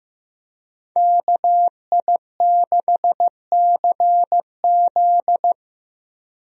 Morse code, heard as a keyed tone, about 15 wpm, 700 Hz. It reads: KI6CZ
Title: USA Call Signs - 5 Characters - Repeated 2x - 15wpm